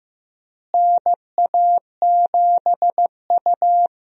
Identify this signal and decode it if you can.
Morse code, heard as a keyed tone, about 15 wpm, 700 Hz. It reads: NA7U
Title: USA Call Signs - 4 Characters - Repeated 2x - 15wpm